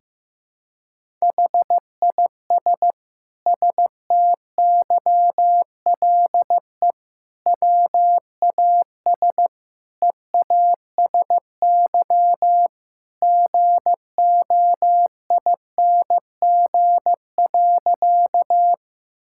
Morse code, heard as a keyed tone, about 16 wpm, 700 Hz. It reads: HIS STYLE WAS EASY GOING.
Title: Sentences from Top 1000 Words - Review Part 5 - 15wpm